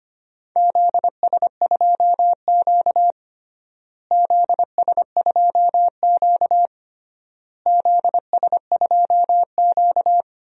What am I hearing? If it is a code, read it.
Morse code, heard as a keyed tone, about 25 wpm, 700 Hz. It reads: ZS2Q ZS2Q ZS2Q
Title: DXCC-100 Call Signs - 4 Characters - Repeated 3x - 25wpm